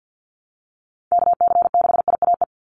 Morse code, heard as a keyed tone, about 50 wpm, 700 Hz. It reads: KC6IAE
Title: USA Call Signs 50wpm